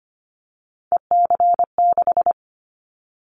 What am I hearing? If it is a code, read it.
Morse code, heard as a keyed tone, about 25 wpm, 700 Hz. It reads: EC6
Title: Callsign Prefixes for Top 100 DXCC - 25wpm